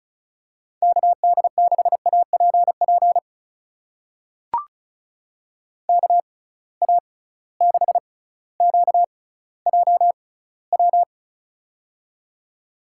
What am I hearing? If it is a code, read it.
Morse code, heard as a keyed tone, about 36 wpm, spaced as for 18 wpm, 700 Hz. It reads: KD6APP KA6QJW
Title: USA Call Signs - 6 Characters - 6x Character Spacing - 35wpm